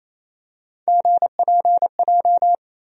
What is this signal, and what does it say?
Morse code, heard as a keyed tone, about 28 wpm, 700 Hz. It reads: GPJ
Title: USA Call Sign Postfixes - 3 Letters 28wpm